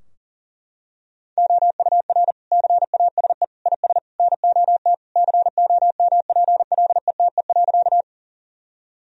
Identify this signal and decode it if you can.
Morse code, heard as a keyed tone, about 40 wpm, 700 Hz. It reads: OUR CASE IS NOT COMPLETE.
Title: Sentences from Top 1000 Words - Review Part 2 - 40wpm